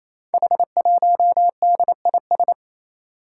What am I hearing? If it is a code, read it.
Morse code, heard as a keyed tone, about 27 wpm, 700 Hz. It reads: H1DIS
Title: Five Challenging Letter-Number Combinations 28wpm